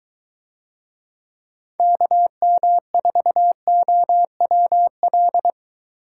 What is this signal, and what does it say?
Morse code, heard as a keyed tone, about 22 wpm, 700 Hz. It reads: KM4OWL